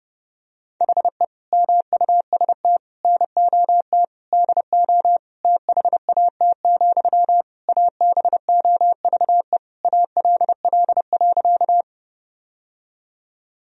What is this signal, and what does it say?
Morse code, heard as a keyed tone, about 30 wpm, 700 Hz. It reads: HE MUST NOT DO THAT, ABOVE ALL.
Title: Sentences from Top 1000 Words - Review Part 2 - 30wpm